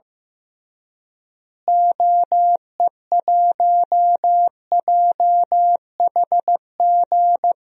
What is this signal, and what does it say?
Morse code, heard as a keyed tone, about 15 wpm, 700 Hz. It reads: OE1JHG
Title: DXCC-100 Call Signs - 6 Characters - Repeated 3x - 15wpm